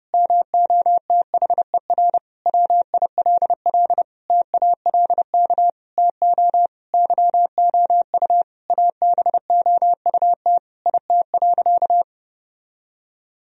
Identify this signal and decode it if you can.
Morse code, heard as a keyed tone, about 32 wpm, 700 Hz. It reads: MOTHER WILL TALK TO YOU ABOUT IT.
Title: Sentences from Top 1000 Words - Review Part 1 - 30wpm